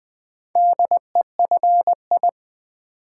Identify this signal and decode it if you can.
Morse code, heard as a keyed tone, about 20 wpm, 700 Hz. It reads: DEFI
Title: Build A Word - 4 Letters At A Time - 10-Letter Words - 20wpm